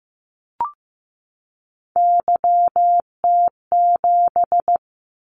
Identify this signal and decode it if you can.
Morse code, heard as a keyed tone, about 15 wpm, 700 Hz. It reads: YT7
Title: Callsign Prefixes for Top 25 DXCC - 15wpm